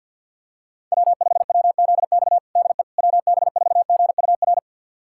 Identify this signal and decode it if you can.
Morse code, heard as a keyed tone, about 47 wpm, 700 Hz. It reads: W5WZX DE WB4GUR
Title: USA Callsign DE Callsign 50wpm